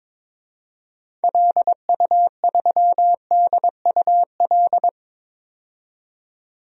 Morse code, heard as a keyed tone, about 22 wpm, 700 Hz. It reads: LU3DUL
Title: DXCC-100 Call Signs - 6 Characters - 22wpm